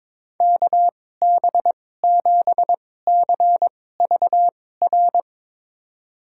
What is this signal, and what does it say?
Morse code, heard as a keyed tone, about 22 wpm, spaced as for 16 wpm, 700 Hz. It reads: KB7CVR
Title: USA Call Signs - 6 Characters - 2x Character Spacing - 22wpm